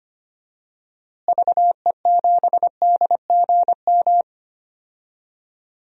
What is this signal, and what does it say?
Morse code, heard as a keyed tone, about 25 wpm, 700 Hz. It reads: VE7DGM
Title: DXCC-100 Call Signs - 6 Characters - Repeated 2x - 25wpm